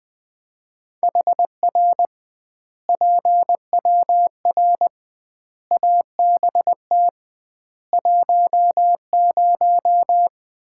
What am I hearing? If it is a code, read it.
Morse code, heard as a keyed tone, about 20 wpm, 700 Hz. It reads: HR PWR ABT 10
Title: Common QSO Phrases - 2x Word-Spacing - 20wpm